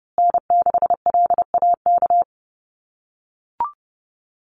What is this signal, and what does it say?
Morse code, heard as a keyed tone, about 30 wpm, 700 Hz. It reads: N6LAK
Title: USA Call Signs Repeated 2x 30wpm